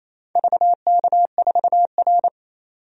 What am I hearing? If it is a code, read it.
Morse code, heard as a keyed tone, about 28 wpm, 700 Hz. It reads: VK4R
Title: DXCC-100 Call Signs - 4 Characters - 28wpm